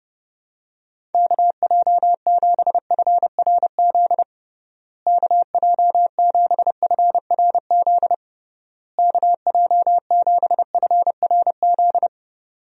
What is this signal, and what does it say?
Morse code, heard as a keyed tone, about 30 wpm, 700 Hz. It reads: KJ7FRZ KJ7FRZ KJ7FRZ
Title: USA Call Signs - 6 Characters - Repeated 3x - 30wpm